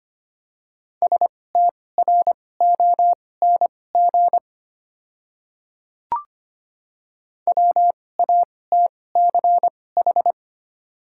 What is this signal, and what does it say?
Morse code, heard as a keyed tone, about 25 wpm, spaced as for 18 wpm, 700 Hz. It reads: STRONG WATCH
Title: 2x Character Spacing - Top 500 Words - 25wpm